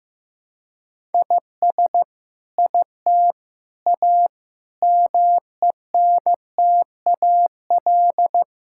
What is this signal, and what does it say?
Morse code, heard as a keyed tone, about 15 wpm, 700 Hz. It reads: IS IT A MENTAL
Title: Sentences from Top 1000 Words - Review Part 6 - 15wpm